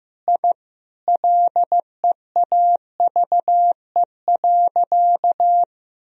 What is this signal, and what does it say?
Morse code, heard as a keyed tone, about 15 wpm, 700 Hz. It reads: I LEAVE.
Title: Sentences from Top 1000 Words - Review Part 2 - 15wpm